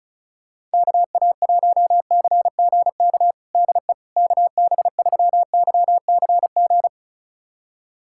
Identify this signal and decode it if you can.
Morse code, heard as a keyed tone, about 36 wpm, 700 Hz. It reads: KA1CGK DE KB3YCG
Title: USA Callsign DE Callsign - 6 Characters - 35wpm